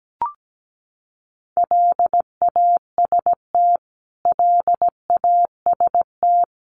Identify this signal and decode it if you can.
Morse code, heard as a keyed tone, about 17 wpm, 700 Hz. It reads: LAST LAST
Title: Top 500 Words Repeated 17wpm